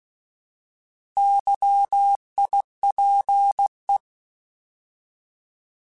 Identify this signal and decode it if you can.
Morse code, heard as a keyed tone, about 16 wpm, 800 Hz. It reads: YIPE